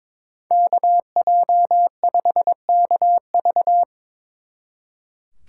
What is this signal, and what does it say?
Morse code, heard as a keyed tone, about 25 wpm, 700 Hz. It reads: KJ5KV
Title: Speed Racing - USA 2x2 Call Signs 22wpm